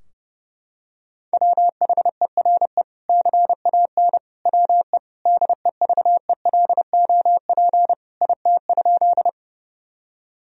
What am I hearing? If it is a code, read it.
Morse code, heard as a keyed tone, about 30 wpm, 700 Hz. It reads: WHERE CAN WE DEVELOP IT?
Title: Sentences from Top 1000 Words - Review Part 2 - 30wpm